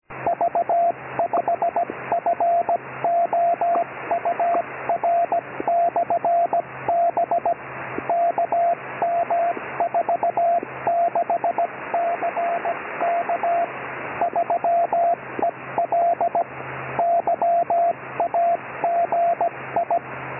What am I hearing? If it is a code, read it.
Morse code, heard as a keyed tone, about 17 wpm, 690 Hz, with radio noise. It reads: V5FOUR/B KM46CK 3EL YAGI
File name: ei4dq / sv5four beacon